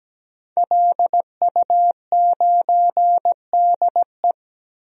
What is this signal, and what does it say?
Morse code, heard as a keyed tone, about 17 wpm, 700 Hz. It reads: LU9DE
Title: DXCC-100 Call Signs - 5 Characters - Repeated 3x - 17wpm